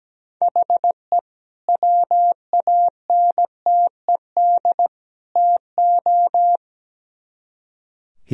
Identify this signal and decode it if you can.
Morse code, heard as a keyed tone, about 20 wpm, 700 Hz. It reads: HE WANTED TO
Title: Speed-Racing - Sets of 3 Words 17wpm